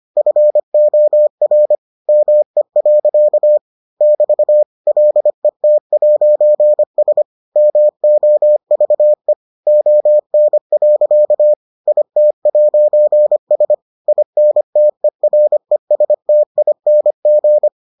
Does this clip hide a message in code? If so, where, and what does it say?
Yes: Morse code from 0.2 to 17.7 s, heard as a keyed tone, about 25 wpm, 600 Hz. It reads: FOR ME. <BT> LET'S MOVE ON. IT'S INTERESTING